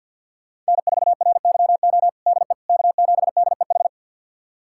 Morse code, heard as a keyed tone, about 51 wpm, 700 Hz. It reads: N4RYO DE K7DEH
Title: USA Callsign DE Callsign - 5 Characters - 50wpm